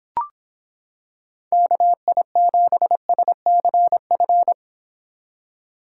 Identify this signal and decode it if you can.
Morse code, heard as a keyed tone, about 26 wpm, 700 Hz. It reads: KI7SCF